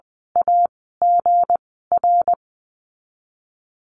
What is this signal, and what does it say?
Morse code, heard as a keyed tone, about 20 wpm, spaced as for 15 wpm, 700 Hz. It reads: AGR
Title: 213 Lesson 34g - Number 8 - Call Signs 20wpm